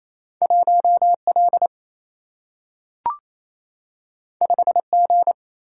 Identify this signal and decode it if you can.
Morse code, heard as a keyed tone, about 28 wpm, 700 Hz. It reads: 1 L 5 G
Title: Single-Letter Single-Number 28wpm